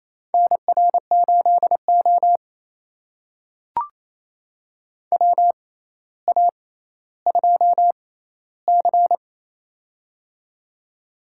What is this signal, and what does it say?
Morse code, heard as a keyed tone, about 27 wpm, spaced as for 14 wpm, 700 Hz. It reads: NR8O WA2C
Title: USA Call Signs - 4 Characters - 6x Character Spacing - 28wpm